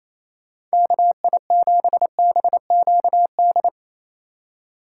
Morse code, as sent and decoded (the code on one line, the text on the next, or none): -.- .. --... -... --.- -..
KI7BQD